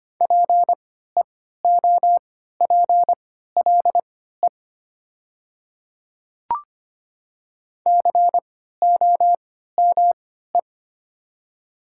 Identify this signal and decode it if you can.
Morse code, heard as a keyed tone, about 25 wpm, spaced as for 14 wpm, 700 Hz. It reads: PEOPLE COME
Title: 3x Character Spacing - Top 300 Words - 25wpm